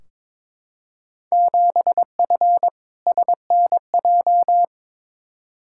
Morse code, heard as a keyed tone, about 21 wpm, 700 Hz. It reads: 7F SNJ
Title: Field Day Exchanges Encore - 22wpm